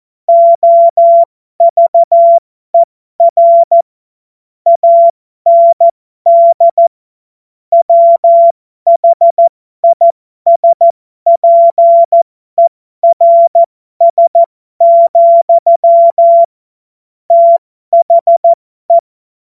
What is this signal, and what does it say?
Morse code, heard as a keyed tone, about 14 wpm, spaced as for 12 wpm, 680 Hz. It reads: OVER AND WHISPERS, THE